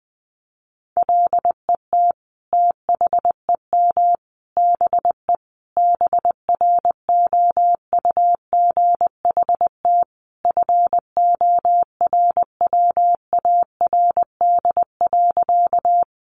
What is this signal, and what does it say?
Morse code, heard as a keyed tone, about 20 wpm, 700 Hz. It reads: LET THEM BE BROUGHT FORWARD.
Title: Sentences from Top 1300 Words 20wpm